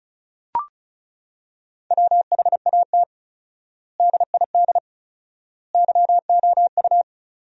Morse code, heard as a keyed tone, about 36 wpm, 700 Hz. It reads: WHAT DID YOU
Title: 4x Word-Spacing - Sentences from Top 100 Words - 35wpm